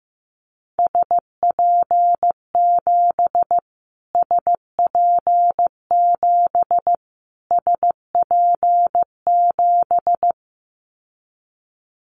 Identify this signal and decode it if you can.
Morse code, heard as a keyed tone, about 15 wpm, 700 Hz. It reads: SP7 SP7 SP7
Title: Callsign Prefixes for Top 25 DXCC - Repeated 3x - 15wpm